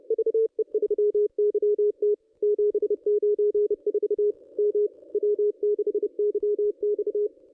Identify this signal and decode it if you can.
Morse code, heard as a keyed tone, about 28 wpm, 440 Hz. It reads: VE3YT 794 M W6YX
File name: contests / ss-cw-k5tr-op-n5rz -1200